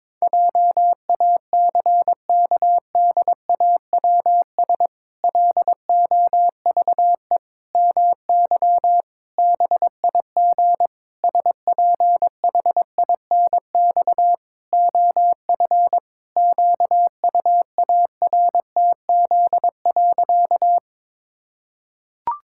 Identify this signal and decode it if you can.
Morse code, heard as a keyed tone, about 22 wpm, 700 Hz. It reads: JACKDAWS LOVE MY BIG SPHINX OF QUARTZ.
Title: Pangram Challenge v2 - 22wpm